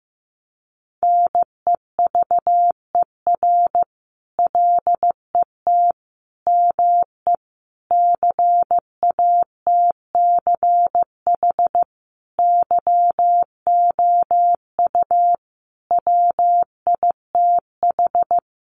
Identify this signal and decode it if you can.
Morse code, heard as a keyed tone, about 15 wpm, 700 Hz. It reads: NEVER LET ME CATCH YOU WITH
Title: Sentences from Top 1500 Words 15wpm